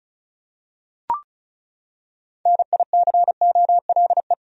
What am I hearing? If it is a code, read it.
Morse code, heard as a keyed tone, about 36 wpm, 700 Hz. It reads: NICOLE
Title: USA Common Names - 35wpm